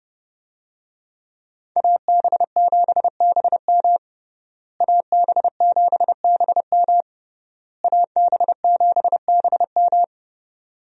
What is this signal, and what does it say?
Morse code, heard as a keyed tone, about 30 wpm, 700 Hz. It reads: AB7BM AB7BM AB7BM
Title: USA Call Signs - 5 Characters - Repeated 3x - 30wpm